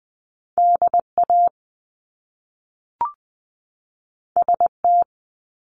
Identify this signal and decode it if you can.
Morse code, heard as a keyed tone, about 20 wpm, 700 Hz. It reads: DA ST
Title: Top Two Letter Combinations in Words 20wpm